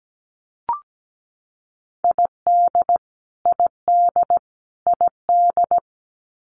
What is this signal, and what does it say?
Morse code, heard as a keyed tone, about 17 wpm, 700 Hz. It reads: ID ID ID
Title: US States Repeated 3x 17wpm